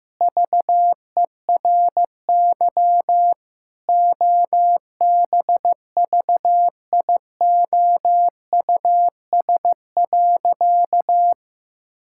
Morse code, heard as a keyed tone, about 15 wpm, 700 Hz. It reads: VERY OBVIOUS.
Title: Sentences from Top 1600 Words 15wpm